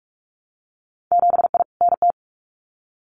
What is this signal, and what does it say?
Morse code, heard as a keyed tone, about 46 wpm, 700 Hz. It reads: 7I NT
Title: Winter Field Day Exchanges Encore 45wpm